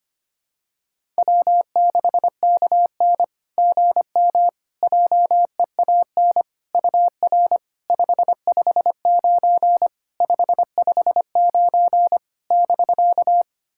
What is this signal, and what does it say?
Morse code, heard as a keyed tone, about 25 wpm, 700 Hz. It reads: W6KN GM JEAN UR 559 559 <BK>
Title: SOTA QSO Practice 25wpm